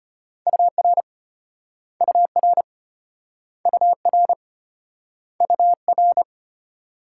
Morse code, heard as a keyed tone, about 32 wpm, 700 Hz. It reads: UR UR UR UR